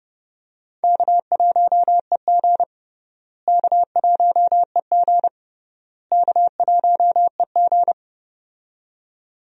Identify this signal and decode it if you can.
Morse code, heard as a keyed tone, about 30 wpm, 700 Hz. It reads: K1EG K1EG K1EG
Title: USA Call Signs - 4 Characters - Repeated 3x - 30wpm